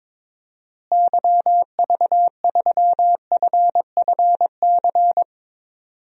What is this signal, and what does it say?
Morse code, heard as a keyed tone, about 22 wpm, 700 Hz. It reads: YV3FFC
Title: DXCC-100 Call Signs - 6 Characters - Repeated 3x - 22wpm